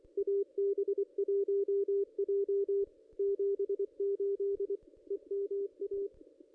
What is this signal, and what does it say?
Morse code, heard as a keyed tone, about 24 wpm, 400 Hz, with radio noise. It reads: AB1J 78 EMA
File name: contests / ss-cw-k5tr-op-n5rz -1600